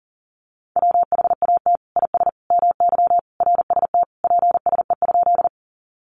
Morse code, heard as a keyed tone, about 40 wpm, 700 Hz. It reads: WHAT IS MY RST PSE?
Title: Common QSO Phrases 40wpm